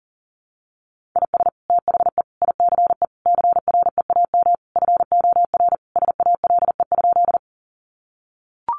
Morse code, heard as a keyed tone, about 40 wpm, 700 Hz. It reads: IS THE ICE CREAM FOR SALE?
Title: Sentences from Top 1500 Words 40wpm